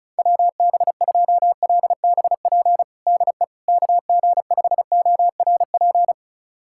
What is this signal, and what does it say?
Morse code, heard as a keyed tone, about 36 wpm, 700 Hz. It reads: WB2LBP DE KG5ORP
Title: USA Callsign DE Callsign - 6 Characters - 35wpm